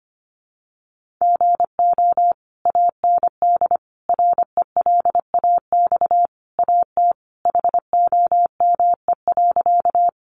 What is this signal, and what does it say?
Morse code, heard as a keyed tone, about 25 wpm, 700 Hz. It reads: GO AND RELAX AT HOME.
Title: Sentences from Top 1400 Words 25wpm